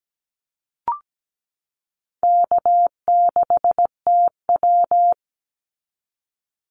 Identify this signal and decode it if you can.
Morse code, heard as a keyed tone, about 17 wpm, 700 Hz. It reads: K6TW
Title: NA SOTA Activator Callsigns 17wpm